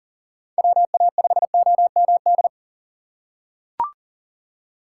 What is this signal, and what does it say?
Morse code, heard as a keyed tone, about 40 wpm, 700 Hz. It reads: WA5OMD
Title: USA Call Signs - 4x Character Spacing - 40wpm